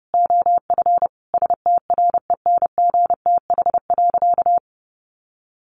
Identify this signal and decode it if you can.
Morse code, heard as a keyed tone, about 30 wpm, 700 Hz. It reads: OF STRENGTH.
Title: Sentences from Top 1300 Words 30wpm